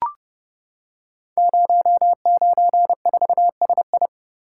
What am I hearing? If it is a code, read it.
Morse code, heard as a keyed tone, about 29 wpm, 700 Hz. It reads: 094SI
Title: Code Groups - 5 Letter:Numbers - 30wpm